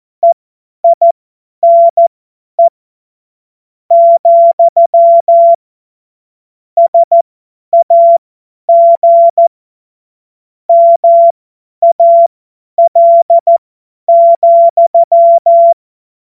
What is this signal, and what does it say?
Morse code, heard as a keyed tone, about 14 wpm, spaced as for 10 wpm, 680 Hz. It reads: EINE , SAG MAL,
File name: Morsekurs / Witze / Witz 31. August 2021, 50 F70